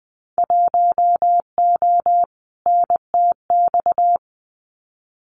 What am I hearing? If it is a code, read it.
Morse code, heard as a keyed tone, about 20 wpm, 700 Hz. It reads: 1O NTX
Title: Winter Field Day Exchanges 20wpm